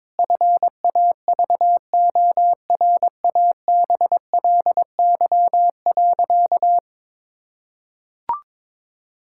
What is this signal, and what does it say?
Morse code, heard as a keyed tone, about 22 wpm, 700 Hz. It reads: FAVORABLY.
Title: Pangram Challenge v2 - 22wpm